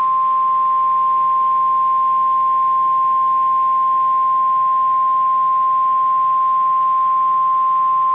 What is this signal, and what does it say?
Emergency Alert System (SAME) tones (the 1050 Hz attention tone). It is heard from 0.9 to 7.5 s.
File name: EAS Logs / KMNR